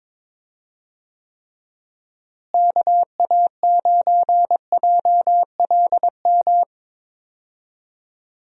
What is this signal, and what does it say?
Morse code, heard as a keyed tone, about 21 wpm, 700 Hz. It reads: KA9JLM
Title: USA Call Signs - 6 Characters - Repeated 2x - 22wpm